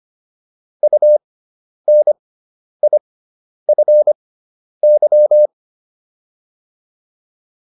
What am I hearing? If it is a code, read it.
Morse code, heard as a keyed tone, about 25 wpm, spaced as for 10 wpm, 600 Hz. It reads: UNIFY